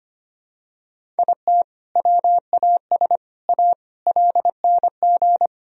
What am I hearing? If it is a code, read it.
Morse code, heard as a keyed tone, about 25 wpm, 700 Hz. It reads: IT WAS A LNG
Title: Sentences from Top 500 Words and Abbreviations 25wpm